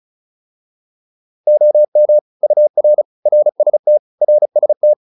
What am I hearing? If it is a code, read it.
Morse code, heard as a keyed tone, about 34 wpm, 600 Hz. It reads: OM UR RST RST
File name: QSO 402 35